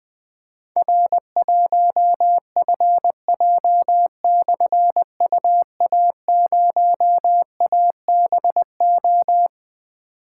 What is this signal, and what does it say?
Morse code, heard as a keyed tone, about 20 wpm, 700 Hz. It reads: R1FJ/UA0ABO
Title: Difficult Call Signs 20wpm